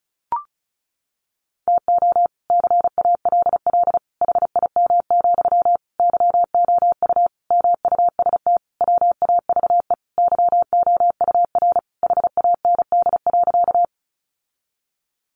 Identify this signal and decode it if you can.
Morse code, heard as a keyed tone, about 34 wpm, 700 Hz. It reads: TO CALL HIM, YOU MUST WAVE YOUR HAND.
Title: Sentences from Top 1400 Words 35wpm